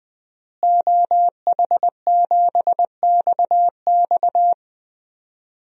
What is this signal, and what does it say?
Morse code, heard as a keyed tone, about 20 wpm, 700 Hz. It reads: OH7XX
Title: DXCC-100 Call Signs - 5 Characters - Repeated 2x - 20wpm